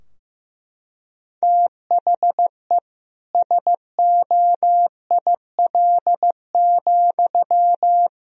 Morse code, heard as a keyed tone, about 15 wpm, 700 Hz. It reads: THE SOIL,
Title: Sentences from Top 1000 Words - Review Part 4 - 15wpm